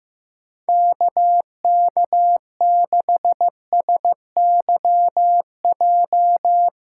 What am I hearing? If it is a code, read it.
Morse code, heard as a keyed tone, about 15 wpm, 700 Hz. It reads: KK6SYJ
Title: USA Callsign DE Callsign - 6 Characters - 15wpm